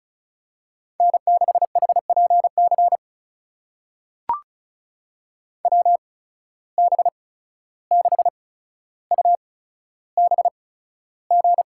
Morse code, heard as a keyed tone, about 34 wpm, spaced as for 14 wpm, 700 Hz. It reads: N6HPC WB6UBG
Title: USA Call Signs - 8x Character Spacing - 35wpm